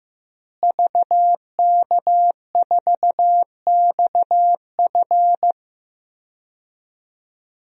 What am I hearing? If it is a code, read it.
Morse code, heard as a keyed tone, about 15 wpm, 700 Hz. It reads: VK4XF
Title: DXCC-100 Call Signs - 5 Characters - Repeated 2x - 15wpm